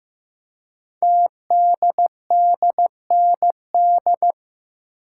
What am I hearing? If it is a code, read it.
Morse code, heard as a keyed tone, about 15 wpm, 700 Hz. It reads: TDDND
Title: Code Groups - 5 Letters - 15wpm